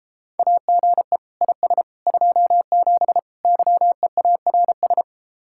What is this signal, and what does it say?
Morse code, heard as a keyed tone, about 33 wpm, 700 Hz. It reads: AGE IS 27 YEARS